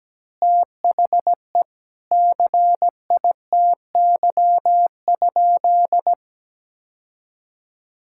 Morse code, heard as a keyed tone, about 17 wpm, 700 Hz. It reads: THE CITY?
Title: Sentences from Top 1600 Words 17wpm